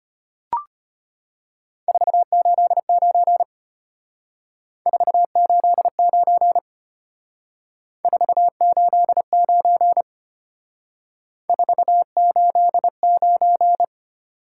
Morse code, heard as a keyed tone, about 32 wpm, 700 Hz. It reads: 489 489 489 489